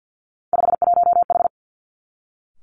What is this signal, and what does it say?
Morse code, heard as a keyed tone, about 50 wpm, 700 Hz. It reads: 51H
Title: Three Challenging Letter-Number Combinations 50wpm